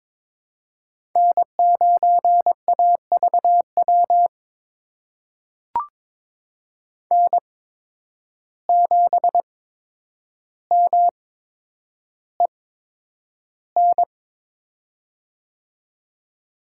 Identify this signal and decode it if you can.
Morse code, heard as a keyed tone, about 22 wpm, spaced as for 9 wpm, 700 Hz. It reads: N9AVW N7MEN
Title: USA Call Signs - 5 Characters - 8x Character Spacing - 22wpm